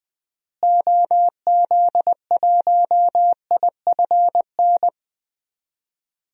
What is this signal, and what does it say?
Morse code, heard as a keyed tone, about 20 wpm, 700 Hz. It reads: OZ1IFN
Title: DXCC-100 Call Signs - 6 Characters - Repeated 3x - 20wpm